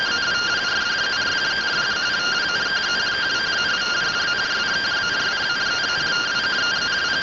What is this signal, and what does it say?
Two-tone teleprinter signal, about 170 Hz shift.
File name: findings / CT AXIAL -377.gen-24 rtty rx